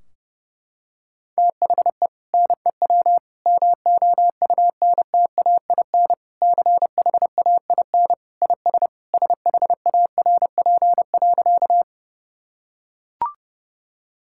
Morse code, heard as a keyed tone, about 32 wpm, 700 Hz. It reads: THE NEW MOUNTAIN CHAIN IS SHARP.
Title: Sentences from Top 1600 Words 30wpm